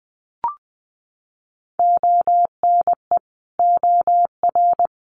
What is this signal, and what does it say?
Morse code, heard as a keyed tone, about 20 wpm, 700 Hz. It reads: ONE OR
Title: Sentences from Top 100 Words 20wpm